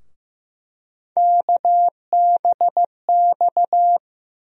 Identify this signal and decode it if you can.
Morse code, heard as a keyed tone, about 15 wpm, 700 Hz. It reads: KBX
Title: USA Call Sign Postfixes - 3 Letters 15wpm